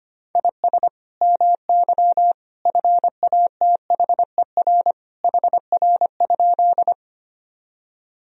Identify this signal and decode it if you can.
Morse code, heard as a keyed tone, about 25 wpm, 700 Hz. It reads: IS MY FATHER HR?
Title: Sentences from Top 300 Words and Abbreviations 25wpm